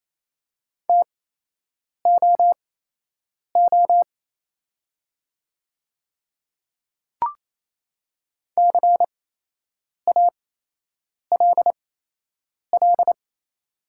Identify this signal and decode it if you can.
Morse code, heard as a keyed tone, about 28 wpm, spaced as for 8 wpm, 700 Hz. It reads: TOO CALL
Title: 8x Character Spacing - Top 200 Words - 28wpm